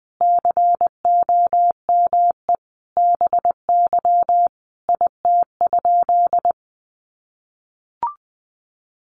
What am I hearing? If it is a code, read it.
Morse code, heard as a keyed tone, about 20 wpm, 700 Hz. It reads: COME BY IT?
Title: Sentences from Top 100 Words 20wpm